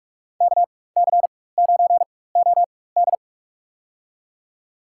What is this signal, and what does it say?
Morse code, heard as a keyed tone, about 46 wpm, spaced as for 21 wpm, 700 Hz. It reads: KC9OD
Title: USA Call Signs - 5 Characters - 4x Character Spacing - 45wpm